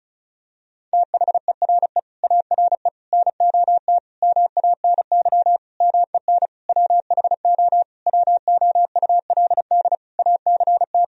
Morse code, heard as a keyed tone, about 34 wpm, 700 Hz. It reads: THERE ARE NOT MANY MEN WHO WOULD ACT
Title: Sentences from Top 1000 Words - Review Part 1 - 35wpm